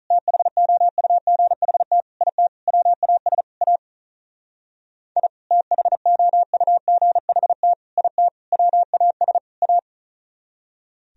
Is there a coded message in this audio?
Yes: Morse code from 0.1 to 9.8 s, heard as a keyed tone, about 45 wpm, 700 Hz. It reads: THOUGHT IT WAS A I THOUGHT IT WAS A